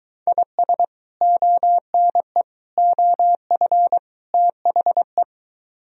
Morse code, heard as a keyed tone, about 23 wpm, 700 Hz. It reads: IS ONE OF THE